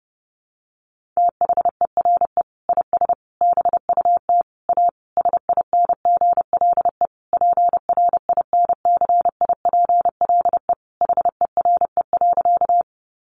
Morse code, heard as a keyed tone, about 30 wpm, 700 Hz. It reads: THERE IS BUT A SINGLE PRINCIPLE HERE.
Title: Sentences from Top 1400 Words 30wpm